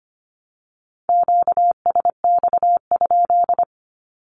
Morse code, heard as a keyed tone, about 25 wpm, 700 Hz. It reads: QSX?
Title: Q-Codes 25wpm